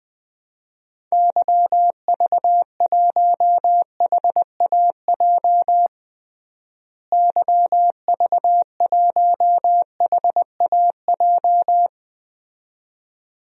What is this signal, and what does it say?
Morse code, heard as a keyed tone, about 20 wpm, 700 Hz. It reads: YV1HAJ YV1HAJ
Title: DXCC-100 Call Signs - 6 Characters - Repeated 2x - 20wpm